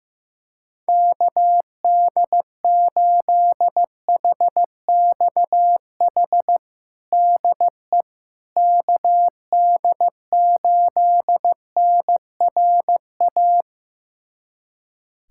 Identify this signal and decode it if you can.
Morse code, heard as a keyed tone, about 15 wpm, 700 Hz. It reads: KD8HXH DE KD8NRA
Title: USA Callsign DE Callsign 15wpm